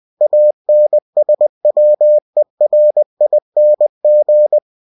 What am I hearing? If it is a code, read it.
Morse code, heard as a keyed tone, about 20 wpm, 600 Hz. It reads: ANSWERING
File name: cw-books / winnie-the-pooh / winnie-20wpm-page-0048